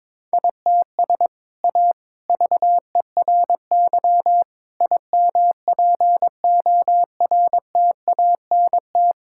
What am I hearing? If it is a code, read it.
Morse code, heard as a keyed tone, about 22 wpm, 700 Hz. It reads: ITS A VERY IMPORTANT
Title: Sentences from Top 1000 Words - Review Part 5 - 22wpm